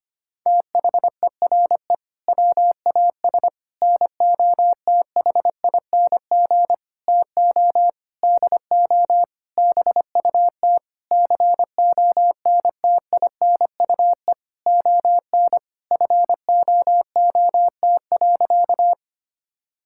Morse code, heard as a keyed tone, about 25 wpm, 700 Hz. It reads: THERE WAS NOTHING TO DO BUT CONTINUE ON FOOT.
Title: Sentences from Top 1000 Words - Review Part 6 - 25wpm